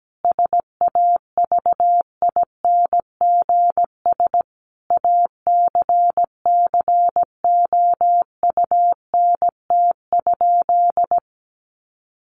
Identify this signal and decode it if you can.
Morse code, heard as a keyed tone, about 17 wpm, 700 Hz. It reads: SAVINGS ACCOUNT?
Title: Sentences from Top 1500 Words 17wpm